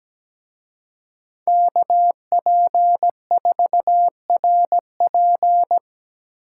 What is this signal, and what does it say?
Morse code, heard as a keyed tone, about 20 wpm, 700 Hz. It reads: KP4RP
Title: Speed Racing - USA 2x2 Call Signs 17wpm